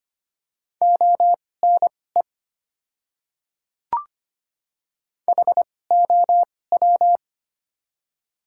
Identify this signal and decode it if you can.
Morse code, heard as a keyed tone, about 25 wpm, spaced as for 18 wpm, 700 Hz. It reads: ONE HOW
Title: 2x Character Spacing - Top 200 Words - 25wpm